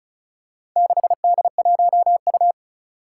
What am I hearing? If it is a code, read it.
Morse code, heard as a keyed tone, about 36 wpm, 700 Hz. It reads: 6D1U
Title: Four Challenging Letter-Number Combinations 35wpm